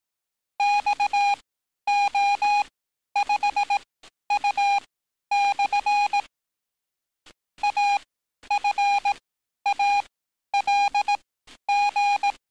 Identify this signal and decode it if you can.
Morse code, heard as a keyed tone, about 18 wpm, spaced as for 11 wpm, 820 Hz. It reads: XO5U/ AFALG
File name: training / NAVPERS 11049to11060 / NAVPERS 11050 RA RB